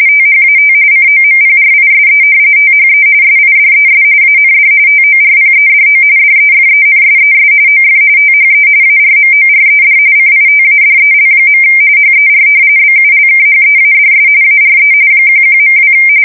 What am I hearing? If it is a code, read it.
RTTY, about 45.45 baud. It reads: THAT COMPRISE THE SIDES AND TORPEDO DEFENSES OF THE ARMORED CITADEL. HER BULKHEADS ARE 11.2
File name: fsk170 / battleshipnj